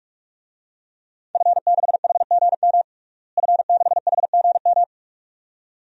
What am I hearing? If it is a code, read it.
Morse code, heard as a keyed tone, about 42 wpm, 700 Hz. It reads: F6HGM F6HGM
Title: DXCC-100 Call Signs - 5 Characters - Repeated 2x - 45wpm